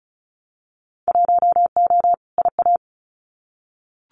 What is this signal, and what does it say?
Morse code, heard as a keyed tone, about 34 wpm, 700 Hz. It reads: 1O IA
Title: Winter Field Day Exchanges 35wpm